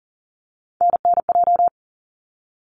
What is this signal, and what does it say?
Morse code, heard as a keyed tone, about 40 wpm, 700 Hz. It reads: NNJ
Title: ARRL Sections 40wpm